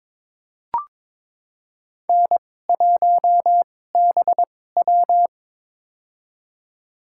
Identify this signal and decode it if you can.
Morse code, heard as a keyed tone, about 21 wpm, spaced as for 16 wpm, 700 Hz. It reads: N1BW
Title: USA Call Signs - 2x Character Spacing - 22wpm